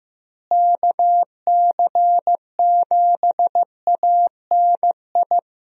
Morse code, heard as a keyed tone, about 15 wpm, 700 Hz. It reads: KC7ANI
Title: USA Call Signs - 6 Characters - Repeated 3x - 15wpm